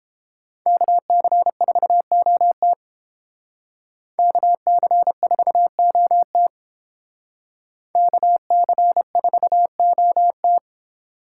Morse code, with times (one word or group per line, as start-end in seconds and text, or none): 0.7-2.7 KC4OT
4.2-6.5 KC4OT
8.0-10.6 KC4OT